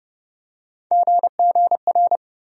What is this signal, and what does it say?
Morse code, heard as a keyed tone, about 30 wpm, 700 Hz. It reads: GGR
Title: USA Call Sign Postfixes - 3 Letters 30wpm